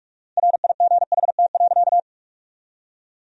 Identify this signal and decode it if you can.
Morse code, heard as a keyed tone, about 42 wpm, 690 Hz. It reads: RIGHT.
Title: 4x Word-Spacing - Sentences from Top 500 Words - 45wpm